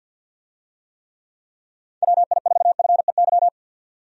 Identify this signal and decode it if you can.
Morse code, heard as a keyed tone, about 48 wpm, 700 Hz. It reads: WI4FEY
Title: USA Call Signs - 6 Characters - 50wpm